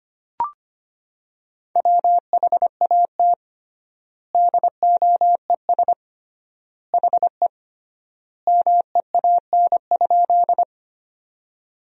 Morse code, heard as a keyed tone, about 25 wpm, 700 Hz. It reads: WHAT DOES HE MEAN?
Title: 3x Word-Spacing - Sentences from Top 200 Words - 25wpm